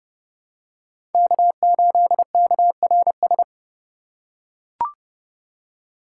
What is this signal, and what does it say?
Morse code, heard as a keyed tone, about 29 wpm, 700 Hz. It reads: K8KRS
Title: USA Call Signs - 5 Characters - 4x Character Spacing - 30wpm